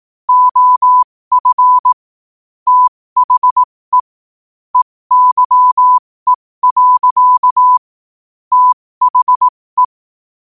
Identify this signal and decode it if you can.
Morse code, heard as a keyed tone, about 18 wpm, spaced as for 15 wpm, 1000 Hz. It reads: OF THE EYE. THE
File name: qotdfiles / 15wpm